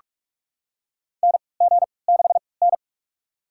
Morse code, heard as a keyed tone, about 41 wpm, spaced as for 25 wpm, 700 Hz. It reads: NG6N
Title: USA Call Signs - 4 Characters - 3x Character Spacing - 45wpm